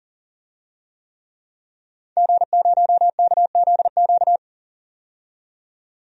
Morse code, heard as a keyed tone, about 40 wpm, 700 Hz. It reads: G0KZQ
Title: DXCC-100 Call Signs - 5 Characters - 40wpm